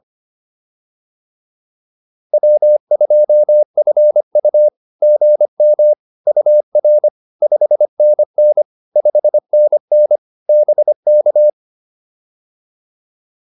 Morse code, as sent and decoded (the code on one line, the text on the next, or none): .-- ..--- ..-. ..- / --. -- / ..- .-. / ..... -. -. / ..... -. -. / -... -.-
W2FU GM UR 5NN 5NN BK